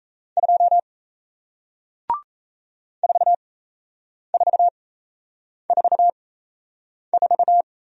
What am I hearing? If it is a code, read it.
Morse code, heard as a keyed tone, about 38 wpm, spaced as for 8 wpm, 700 Hz. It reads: 2 4444